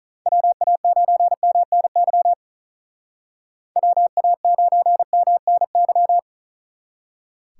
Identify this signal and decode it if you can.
Morse code, heard as a keyed tone, about 45 wpm, 700 Hz. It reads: WA9MNY WA9MNY